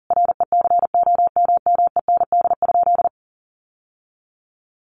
Morse code, heard as a keyed tone, about 40 wpm, 700 Hz. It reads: RECOMMEND?
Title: Sentences from Top 1200 Words 40wpm